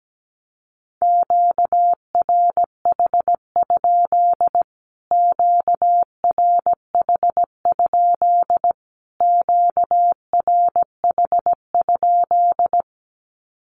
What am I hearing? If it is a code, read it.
Morse code, heard as a keyed tone, about 17 wpm, 700 Hz. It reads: QRH? QRH? QRH?
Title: Q-Codes Repeated 3x 17wpm